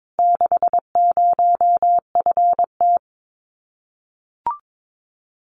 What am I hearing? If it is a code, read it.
Morse code, heard as a keyed tone, about 22 wpm, 700 Hz. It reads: 60FT
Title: Common QSO Phrases Encore 22wpm